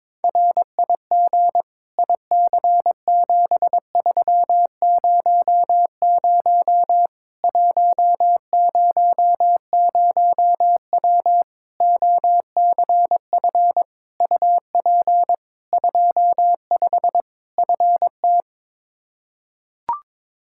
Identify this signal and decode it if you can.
Morse code, heard as a keyed tone, about 22 wpm, 700 Hz. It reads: RIG IC7300 100W OCF UP 25 FT
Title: Practice QSO - Exchange 2 - Info - 22wpm